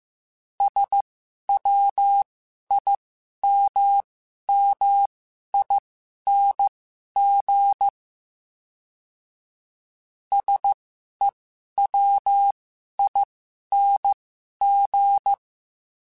Morse code, heard as a keyed tone, about 15 wpm, spaced as for 11 wpm, 800 Hz. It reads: SWIMMING SEWING
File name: Special INGs-mp3s / Special INGs-mp3s / Sngl Ing @10